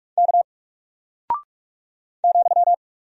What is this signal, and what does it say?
Morse code, heard as a keyed tone, about 44 wpm, 700 Hz. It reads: K ,
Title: Single Character (Letter:Number:Punctuation:Prosign) 45wpm